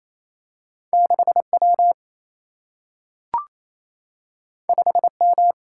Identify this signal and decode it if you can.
Morse code, heard as a keyed tone, about 28 wpm, 700 Hz. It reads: 6 W 5 M
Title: Single-Letter Single-Number 28wpm